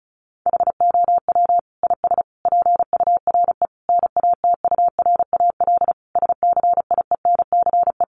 Morse code, heard as a keyed tone, about 34 wpm, 700 Hz. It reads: HOW IS PURE NATURAL SCIENCE
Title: Sentences from Top 1500 Words 35wpm